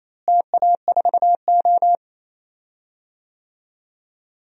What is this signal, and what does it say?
Morse code, heard as a keyed tone, about 28 wpm, 700 Hz. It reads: TA4O
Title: DXCC-100 Call Signs - 4 Characters - Repeated 3x - 28wpm